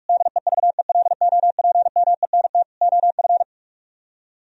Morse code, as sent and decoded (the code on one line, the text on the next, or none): -.. ...- . .-.. --- .--. -- . -. - / --- ..-.
DEVELOPMENT OF